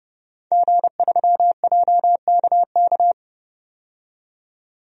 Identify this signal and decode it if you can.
Morse code, heard as a keyed tone, about 30 wpm, 700 Hz. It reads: G3JKK
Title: DXCC-100 Call Signs - 5 Characters - 30wpm